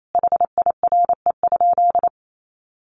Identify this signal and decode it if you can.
Morse code, heard as a keyed tone, about 28 wpm, 700 Hz. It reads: HIRE?
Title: Sentences from Top 1400 Words 28wpm